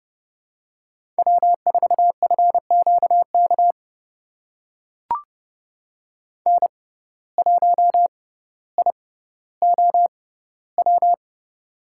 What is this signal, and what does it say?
Morse code, heard as a keyed tone, about 31 wpm, spaced as for 15 wpm, 700 Hz. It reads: W4FQK N1IOW
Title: USA Call Signs - 6x Character Spacing - 30wpm